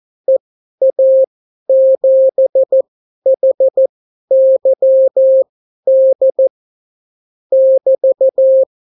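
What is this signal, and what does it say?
Morse code, heard as a keyed tone, about 14 wpm, spaced as for 11 wpm, 530 Hz. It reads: EA7HYD <BT>
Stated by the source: Example Spanish CW QSO from a Morse learning book, 14 wpm characters with 8 wpm spacing.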